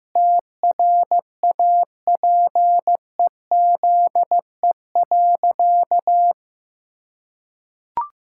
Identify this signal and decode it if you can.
Morse code, heard as a keyed tone, about 15 wpm, 700 Hz. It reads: TRAPEZE.
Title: Pangram Challenge v2 - 15wpm